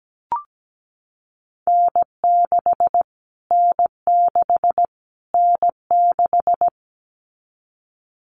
Morse code, heard as a keyed tone, about 17 wpm, 700 Hz. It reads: N6 N6 N6
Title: Callsign Prefixes for Top 25 DXCC - Repeated 3x - 17wpm